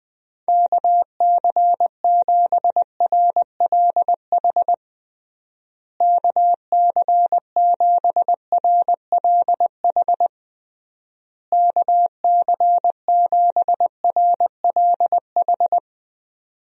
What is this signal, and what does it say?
Morse code, heard as a keyed tone, about 20 wpm, 700 Hz. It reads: KC7RLH KC7RLH KC7RLH
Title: USA Call Signs - 6 Characters - Repeated 3x - 20wpm